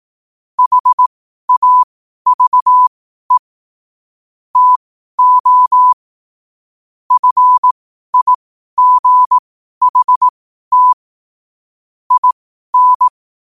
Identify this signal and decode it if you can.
Morse code, heard as a keyed tone, about 18 wpm, spaced as for 12 wpm, 1000 Hz. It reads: HAVE TO FIGHT IN
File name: qotdfiles / 12wpm